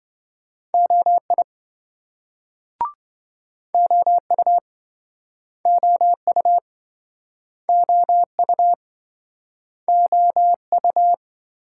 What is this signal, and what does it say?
Morse code, heard as a keyed tone, about 25 wpm, 700 Hz. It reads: OI OU OU OU OU
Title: Speed-Racing Multi-letter Phonemes - 20wpm